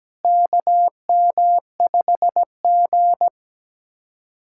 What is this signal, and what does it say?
Morse code, heard as a keyed tone, about 17 wpm, 700 Hz. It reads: KM5G
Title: USA Callsign DE Callsign - 4 Characters - 17wpm